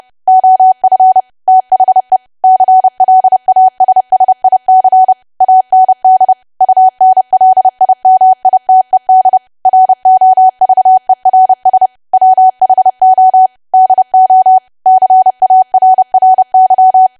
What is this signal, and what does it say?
Morse code, heard as a keyed tone, about 30 wpm, 750 Hz. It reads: OF THE CLASSIC AND UNLIMITED ROVERS WHO DO CARRY
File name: cwa / W1AW-Level3 / 30WPM